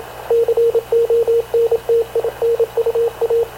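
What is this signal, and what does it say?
Morse code, heard as a keyed tone, about 28 wpm, 460 Hz, with radio noise. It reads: CONTINUA
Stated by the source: Havana Radio / CLA sending Spanish press (PX) in Morse, recorded by Radio Officer Paul Dumesnil onboard CNAV QUEST/CZDO.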